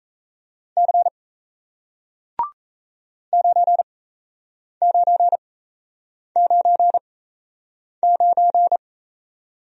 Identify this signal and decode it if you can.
Morse code, heard as a keyed tone, about 37 wpm, spaced as for 8 wpm, 700 Hz. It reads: C 9999